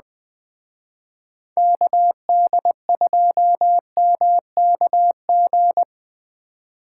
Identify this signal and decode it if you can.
Morse code, heard as a keyed tone, about 23 wpm, 700 Hz. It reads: KD2MKG
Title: Speed Racing - USA 2x3 Call Signs 20wpm